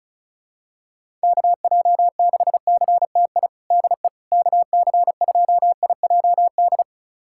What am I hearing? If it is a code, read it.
Morse code, heard as a keyed tone, about 36 wpm, 700 Hz. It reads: KJ6CTI DE KC2IJD
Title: USA Callsign DE Callsign - 6 Characters - 35wpm